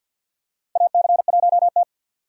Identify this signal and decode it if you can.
Morse code, heard as a keyed tone, about 47 wpm, 700 Hz. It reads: AC1T
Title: USA Call Signs - 2x Character Spacing - 50wpm